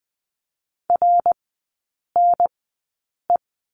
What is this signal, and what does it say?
Morse code, heard as a keyed tone, about 20 wpm, spaced as for 8 wpm, 700 Hz. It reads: RNE
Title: 053 Lesson 10e - Letter r - Three Characters 20wpm